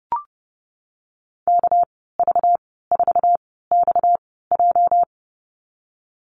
Morse code, heard as a keyed tone, about 30 wpm, spaced as for 17 wpm, 700 Hz. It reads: KV4XJ
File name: Lesson 37g - Letter x - Call Signs 30wpm